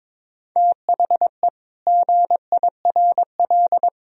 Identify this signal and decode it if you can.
Morse code, heard as a keyed tone, about 22 wpm, 700 Hz. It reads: THE GIRL
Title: Sentences from Top 1000 Words - Review Part 4 - 22wpm